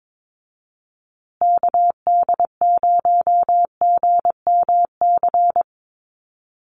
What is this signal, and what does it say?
Morse code, heard as a keyed tone, about 22 wpm, 700 Hz. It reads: KD0GMC
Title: USA Call Signs Repeated 3x 22wpm